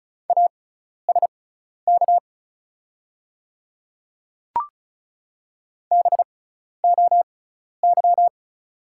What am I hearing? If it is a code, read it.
Morse code, heard as a keyed tone, about 36 wpm, spaced as for 12 wpm, 690 Hz. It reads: ASK BOY
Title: 6x Character Spacing - Top 200 Words - 35wpm